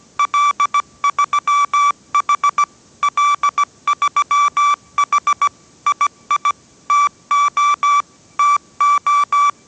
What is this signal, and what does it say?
Morse code, heard as a keyed tone, about 17 wpm, 1160 Hz. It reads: L3H L3H II TO TO
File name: 18wpm